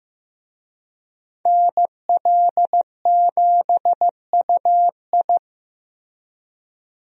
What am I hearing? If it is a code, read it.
Morse code, heard as a keyed tone, about 15 wpm, 700 Hz. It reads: NL7UI
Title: USA Call Signs - 5 Characters - Repeated 2x - 15wpm